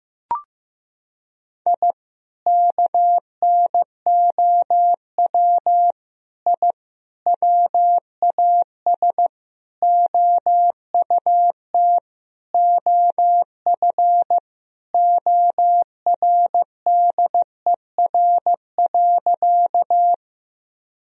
Morse code, heard as a keyed tone, about 15 wpm, 700 Hz. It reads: I KNOW I WAS OUT OF ORDER.
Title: Sentences from Top 1000 Words - Review Part 2 - 15wpm